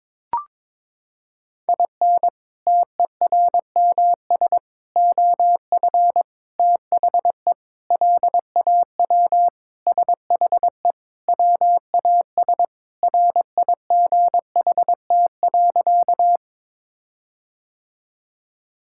Morse code, heard as a keyed tone, about 22 wpm, 700 Hz. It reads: IN TERMS OF THE LAW SHE WAS RIGHT.
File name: practice / 22wpm / Sentences with 4-Word Phrases - 22wpm